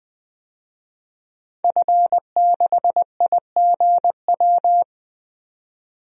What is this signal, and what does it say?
Morse code, heard as a keyed tone, about 21 wpm, 700 Hz. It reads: F6IGW
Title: DXCC-100 Call Signs - 5 Characters - 20wpm